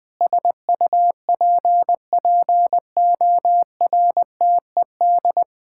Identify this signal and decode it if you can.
Morse code, heard as a keyed tone, about 21 wpm, 700 Hz. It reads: SUPPORTED
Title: Sentences with 6-Word Phrases - Spoken Beginning - 20wpm